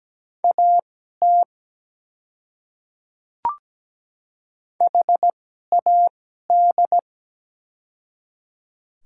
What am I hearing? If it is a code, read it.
Morse code, heard as a keyed tone, about 17 wpm, spaced as for 12 wpm, 700 Hz. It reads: AT HAD
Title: 2x Character Spacing - Top 200 Words - 17wpm